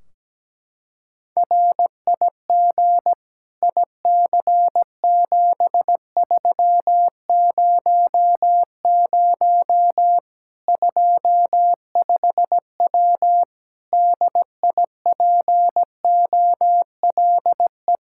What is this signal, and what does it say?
Morse code, heard as a keyed tone, about 17 wpm, 700 Hz. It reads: RIG IC7300 25W DIPOLE
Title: K1USN SST QSO Practice 17wpm